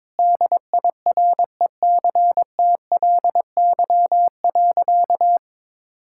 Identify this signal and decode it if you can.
Morse code, heard as a keyed tone, about 22 wpm, 700 Hz. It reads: DIRECTLY.
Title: Sentences from Top 1000 Words - Review Part 5 - 22wpm